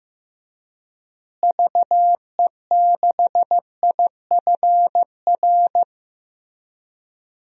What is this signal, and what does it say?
Morse code, heard as a keyed tone, about 15 wpm, 700 Hz. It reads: VE6IFR
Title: DXCC-100 Call Signs - 6 Characters - Repeated 3x - 15wpm